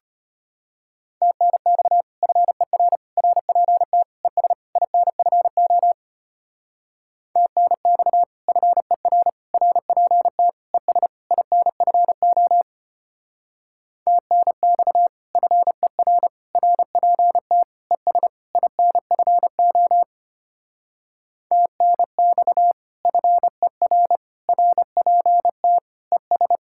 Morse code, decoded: TNX FER RPT ES INFO TNX FER RPT ES INFO TNX FER RPT ES INFO TNX FER RPT ES

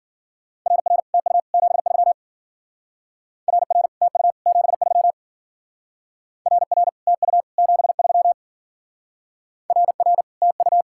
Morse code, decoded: RR TU 73 RR TU 73 RR TU 73 RR TU

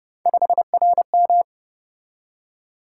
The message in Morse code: 5RM

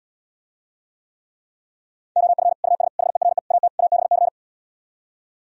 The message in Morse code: OF US HERE AT ABC

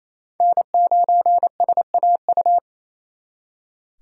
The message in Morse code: N9SAU

